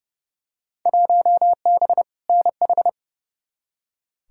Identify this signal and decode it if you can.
Morse code, heard as a keyed tone, about 30 wpm, 700 Hz. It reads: 1B NH
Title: Field Day Exchanges Encore - 30wpm